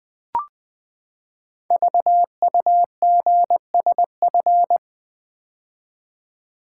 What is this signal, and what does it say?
Morse code, heard as a keyed tone, about 20 wpm, 700 Hz. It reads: VUGSF
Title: Code Groups - 5 Letters - 20wpm